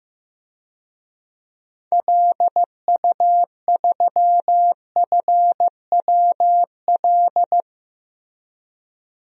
Morse code, heard as a keyed tone, about 16 wpm, 700 Hz. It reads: LU3FWL
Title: DXCC-100 Call Signs - 6 Characters - 15wpm